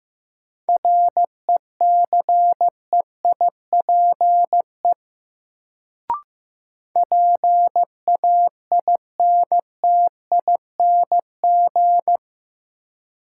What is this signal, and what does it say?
Morse code, heard as a keyed tone, about 15 wpm, 700 Hz. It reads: RECEIPE PAINTING
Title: Sentences from Top 1200 Words - New Word Drill Alt - 15wpm